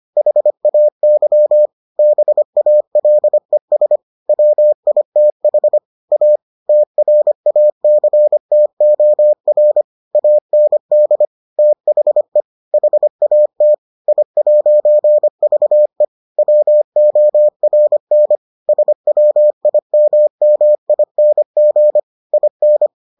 Morse code, heard as a keyed tone, about 25 wpm, 600 Hz. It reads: HAY BALES WITH A TRACTOR AND THE HAT I'VE WORN SWIMMING IN